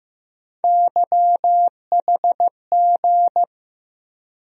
Morse code, heard as a keyed tone, about 15 wpm, 700 Hz. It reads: YHG
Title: USA Call Sign Postfixes - 3 Letters 15wpm